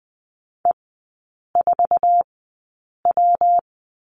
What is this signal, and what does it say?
Morse code, heard as a keyed tone, about 19 wpm, 700 Hz. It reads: E 4 W
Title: 114 Lesson 19e - Letter w - Three Characters 20wpm